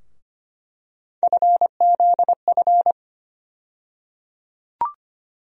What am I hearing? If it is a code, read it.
Morse code, heard as a keyed tone, about 25 wpm, 700 Hz. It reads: FZF
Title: USA Call Sign Postfixes - 3 Letters 25wpm